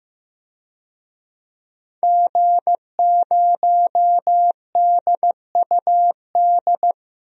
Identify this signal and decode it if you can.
Morse code, heard as a keyed tone, about 15 wpm, 700 Hz. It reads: G0DUD
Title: DXCC-100 Call Signs - 5 Characters - 15wpm